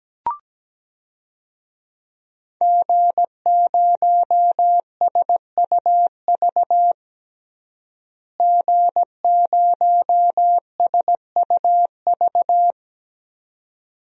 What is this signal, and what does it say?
Morse code, heard as a keyed tone, about 17 wpm, 700 Hz. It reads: G0SUV G0SUV
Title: DXCC-100 Call Signs - 5 Characters - Repeated 2x - 17wpm